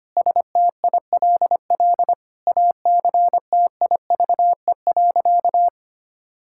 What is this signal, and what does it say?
Morse code, heard as a keyed tone, about 25 wpm, 700 Hz. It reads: STILL ACTIVE.
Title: Sentences from Top 1000 Words - Review Part 6 - 25wpm